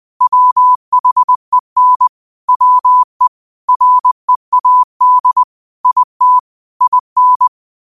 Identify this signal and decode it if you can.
Morse code, heard as a keyed tone, about 20 wpm, 1000 Hz. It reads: WHEN WE READ IT IN
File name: qotdfiles / 20wpm